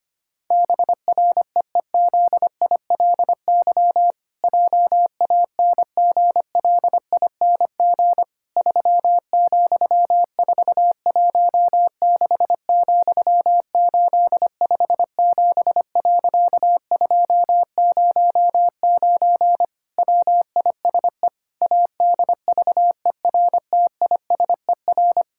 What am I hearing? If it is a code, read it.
Morse code, heard as a keyed tone, about 25 wpm, 700 Hz. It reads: BREEZILY JANGLING 3,416,857.209 WISE ADVERTISER
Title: Pangram Challenge v2 - 25wpm